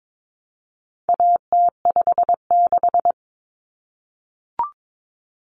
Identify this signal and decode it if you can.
Morse code, heard as a keyed tone, about 22 wpm, 700 Hz. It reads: AT56
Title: Common Cut Numbers - 4-Digit 22wpm